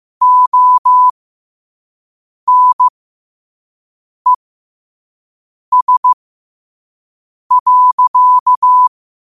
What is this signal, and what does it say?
Morse code, heard as a keyed tone, about 15 wpm, spaced as for 5 wpm, 1000 Hz. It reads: ONES.
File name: qotdfiles / 5wpm